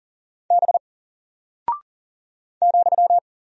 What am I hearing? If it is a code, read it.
Morse code, heard as a keyed tone, about 38 wpm, 700 Hz. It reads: B ,
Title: Single Character (Letter:Number:Punctuation:Prosign) Alt 40wpm